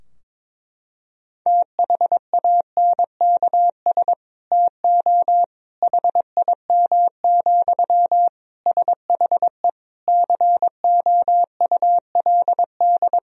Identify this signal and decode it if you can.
Morse code, heard as a keyed tone, about 22 wpm, 700 Hz. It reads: THANKS TO HIM, SHE COULD
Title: Sentences from Top 1000 Words - Review Part 3 - 22wpm